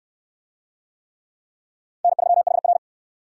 Morse code, heard as a keyed tone, about 68 wpm, 700 Hz. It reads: N3HR